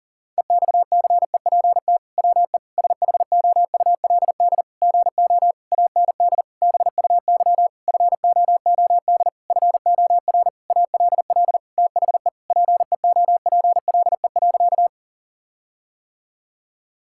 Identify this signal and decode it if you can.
Morse code, heard as a keyed tone, about 40 wpm, 700 Hz. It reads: EXCEPT WE SHOULD GO AND BUY FOOD FOR ALL THE PEOPLE.
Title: Sentences from Top 1600 Words 40wpm